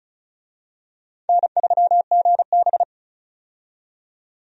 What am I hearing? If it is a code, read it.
Morse code, heard as a keyed tone, about 36 wpm, 700 Hz. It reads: N3GB
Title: USA Call Signs - 4 Characters - 35wpm